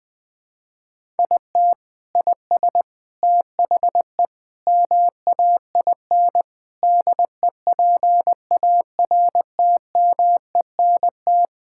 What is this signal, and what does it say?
Morse code, heard as a keyed tone, about 21 wpm, 700 Hz. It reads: IT IS THE MAIN DEPARTMENT
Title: Sentences from Top 1000 Words - Review Part 5 - 20wpm